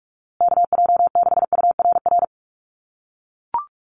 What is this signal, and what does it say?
Morse code, heard as a keyed tone, about 44 wpm, 700 Hz. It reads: KJ6URR
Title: USA Call Signs Repeated 2x 45wpm